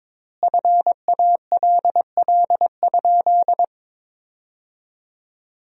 Morse code, heard as a keyed tone, about 22 wpm, 700 Hz. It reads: FALL?
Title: 4x Word-Spacing - Sentences from Top 500 Words - 22wpm